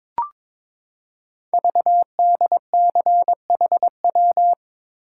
Morse code, heard as a keyed tone, about 22 wpm, 690 Hz. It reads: VDCHW
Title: Code Groups - 5 Letters - 22wpm